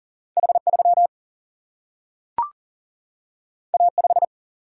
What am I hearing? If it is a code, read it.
Morse code, heard as a keyed tone, about 40 wpm, 700 Hz. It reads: H 3 A 5
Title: Single-Letter Single-Number 40wpm